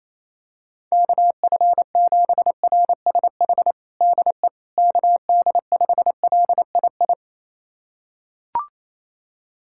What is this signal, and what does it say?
Morse code, heard as a keyed tone, about 28 wpm, 700 Hz. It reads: KF7RSH DE KD5LII
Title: USA Callsign DE Callsign 28wpm